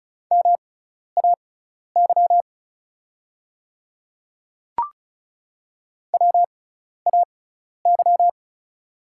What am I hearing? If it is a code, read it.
Morse code, heard as a keyed tone, about 34 wpm, spaced as for 12 wpm, 700 Hz. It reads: MAY WAY
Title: 6x Character Spacing - Top 100 Words - 35wpm